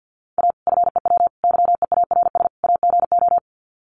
Morse code, heard as a keyed tone, about 50 wpm, 700 Hz. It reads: A FEW YEARS AGO
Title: Sets of 4 Words 50wpm